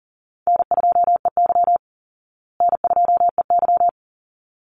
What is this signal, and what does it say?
Morse code, heard as a keyed tone, about 40 wpm, 700 Hz. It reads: N2EY N2EY
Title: Call Signs Repeated 3x - 40wpm